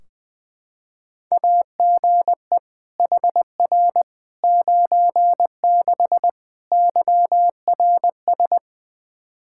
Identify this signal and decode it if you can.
Morse code, heard as a keyed tone, about 20 wpm, 700 Hz. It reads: AGE HR 96 YRS
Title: Practice QSO - Exchange 3 - More Info - 20wpm